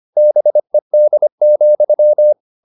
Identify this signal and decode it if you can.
Morse code, heard as a keyed tone, about 25 wpm, 600 Hz. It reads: BED,
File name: the old man and the sea part 1 25wpm